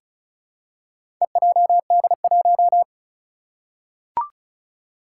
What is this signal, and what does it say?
Morse code, heard as a keyed tone, about 35 wpm, 700 Hz. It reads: EJD1
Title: Four Challenging Letter-Number Combinations 35wpm